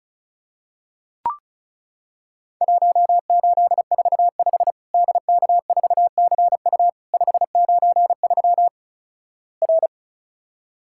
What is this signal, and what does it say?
Morse code, heard as a keyed tone, about 36 wpm, 700 Hz. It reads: 1845 DK4CU 593 R
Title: Worked All Europe DX Contest - QTC Exchange - 35wpm